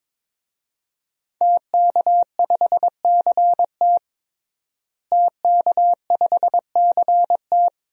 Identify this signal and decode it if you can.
Morse code, heard as a keyed tone, about 22 wpm, 700 Hz. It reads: TK5CT TK5CT
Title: DXCC-100 Call Signs - 5 Characters - Repeated 2x - 22wpm